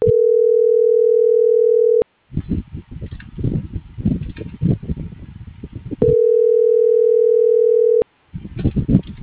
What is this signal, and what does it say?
Ringback tone (North America), 440 + 480 Hz.